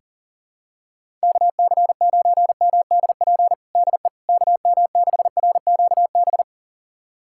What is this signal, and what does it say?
Morse code, heard as a keyed tone, about 40 wpm, 700 Hz. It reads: KC9MDP DE KM6RQB
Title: USA Callsign DE Callsign - 6 Characters - 40wpm